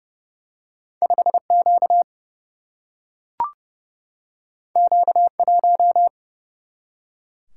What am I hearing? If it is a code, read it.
Morse code, heard as a keyed tone, about 30 wpm, 700 Hz. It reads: 5 Q Q 1
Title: Single-Letter Single-Number 30wpm